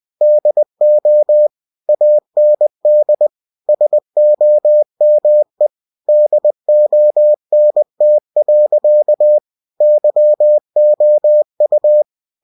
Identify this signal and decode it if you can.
Morse code, heard as a keyed tone, about 20 wpm, 600 Hz. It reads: DO AND SOME DONT. YOU
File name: cw-books / winnie-the-pooh / winnie-20wpm-page-0038